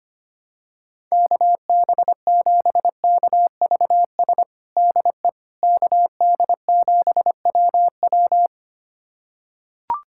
Morse code, heard as a keyed tone, about 25 wpm, 700 Hz. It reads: KB7KVS DE KD7WW
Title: USA Callsign DE Callsign 25wpm